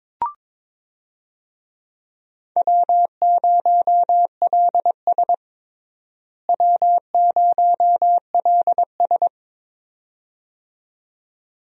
Morse code, heard as a keyed tone, about 22 wpm, 700 Hz. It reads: W0LS W0LS
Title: USA Call Signs - 4 Characters - Repeated 2x - 22wpm